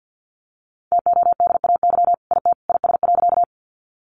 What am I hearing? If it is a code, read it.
Morse code, heard as a keyed tone, about 50 wpm, 700 Hz. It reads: TODAY IT IS.
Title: Sentences from Top 600 Words 50wpm